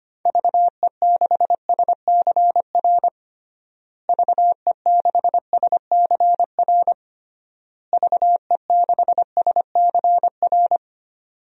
Morse code, heard as a keyed tone, about 25 wpm, 700 Hz. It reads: VE6SCR VE6SCR VE6SCR
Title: DXCC-100 Call Signs - 6 Characters - Repeated 3x - 25wpm